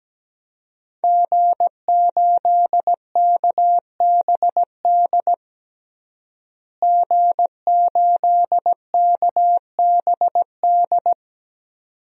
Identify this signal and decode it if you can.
Morse code, heard as a keyed tone, about 17 wpm, 700 Hz. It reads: G8KBD G8KBD
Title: DXCC-100 Call Signs - 5 Characters - Repeated 2x - 17wpm